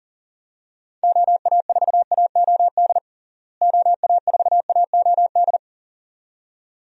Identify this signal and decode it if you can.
Morse code, heard as a keyed tone, about 40 wpm, 700 Hz. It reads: OA4AOD OA4AOD
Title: USA Call Signs - 6 Characters - Repeated 2x - 40wpm